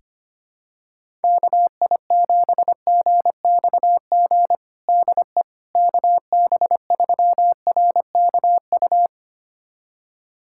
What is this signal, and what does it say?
Morse code, heard as a keyed tone, about 25 wpm, 700 Hz. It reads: KI7GXG DE KB3RKU
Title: USA Callsign DE Callsign 25wpm